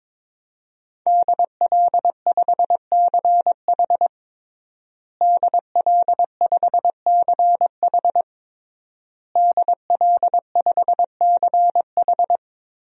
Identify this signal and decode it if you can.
Morse code, heard as a keyed tone, about 22 wpm, 700 Hz. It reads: DL5CH DL5CH DL5CH
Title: DXCC-100 Call Signs - 5 Characters - Repeated 3x - 22wpm